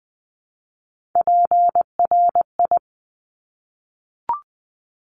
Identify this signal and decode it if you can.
Morse code, heard as a keyed tone, about 20 wpm, 700 Hz. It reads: PRI
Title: Top Three Letter Combinations in Words 20wpm